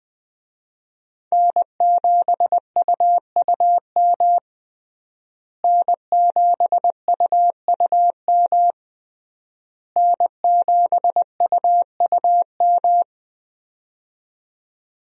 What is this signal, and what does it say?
Morse code, heard as a keyed tone, about 20 wpm, 700 Hz. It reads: N7UUM N7UUM N7UUM
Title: USA Call Signs - 5 Characters - Repeated 3x - 20wpm